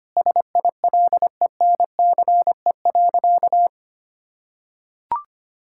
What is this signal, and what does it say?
Morse code, heard as a keyed tone, about 25 wpm, 700 Hz. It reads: SILENCE.
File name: practice / 25wpm / Sentences with 3-Word Phrases - 25wpm